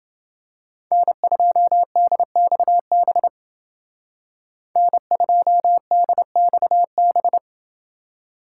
Morse code, heard as a keyed tone, about 29 wpm, 700 Hz. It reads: N2DXB N2DXB